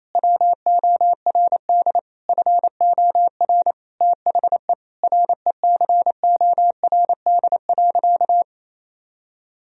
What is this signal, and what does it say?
Morse code, heard as a keyed tone, about 28 wpm, 700 Hz. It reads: WORD FOR THE RECORD.
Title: Sentences from Top 1000 Words - Review Part 3 - 28wpm